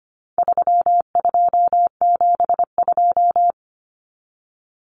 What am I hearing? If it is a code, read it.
Morse code, heard as a keyed tone, about 25 wpm, 700 Hz. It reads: 3272
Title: Common Cut Numbers - 4-Digit 25wpm